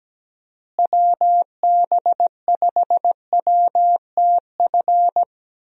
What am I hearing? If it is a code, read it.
Morse code, heard as a keyed tone, about 17 wpm, 700 Hz. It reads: WB5WTF
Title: USA Call Signs - 6 Characters - Repeated 2x - 17wpm